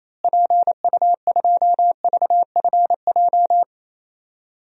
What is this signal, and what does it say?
Morse code, heard as a keyed tone, about 28 wpm, 700 Hz. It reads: PU2VFJ
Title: DXCC-100 Call Signs - 6 Characters - Repeated 2x - 28wpm